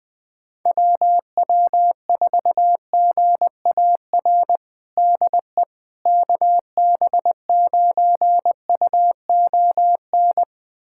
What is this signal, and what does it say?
Morse code, heard as a keyed tone, about 20 wpm, 700 Hz. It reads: WW4GAR DE KB9UON
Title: USA Callsign DE Callsign - 6 Characters - 20wpm